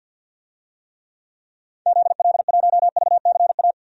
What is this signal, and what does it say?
Morse code, heard as a keyed tone, about 52 wpm, 700 Hz. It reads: ZL1VCA
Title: DXCC-100 Call Signs - 6 Characters - 50wpm